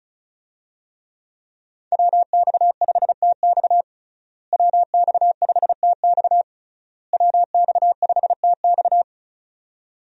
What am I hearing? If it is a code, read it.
Morse code, heard as a keyed tone, about 36 wpm, 700 Hz. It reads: WX5TX WX5TX WX5TX
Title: USA Call Signs - 5 Characters - Repeated 3x - 35wpm